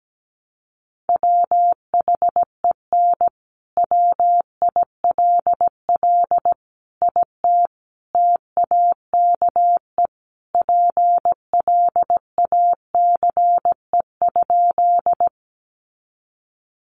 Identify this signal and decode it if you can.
Morse code, heard as a keyed tone, about 17 wpm, 700 Hz. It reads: WHEN WILL IT TAKE PLACE?
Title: Sentences from Top 200 Words 17wpm